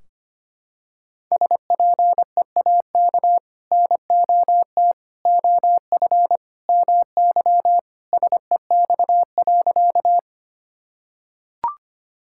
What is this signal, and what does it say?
Morse code, heard as a keyed tone, about 25 wpm, 700 Hz. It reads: SPEAK NOT OF MY SEX.
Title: Sentences from Top 1600 Words 25wpm